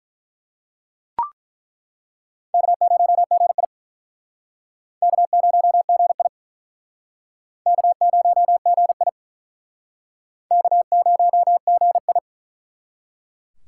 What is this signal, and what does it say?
Morse code, heard as a keyed tone, about 47 wpm, 700 Hz. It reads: K0GI K0GI K0GI K0GI